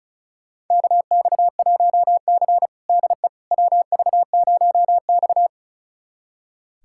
Morse code, heard as a keyed tone, about 36 wpm, 700 Hz. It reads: KX1C DE WV0X
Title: USA Callsign DE Callsign - 4 Characters - 35wpm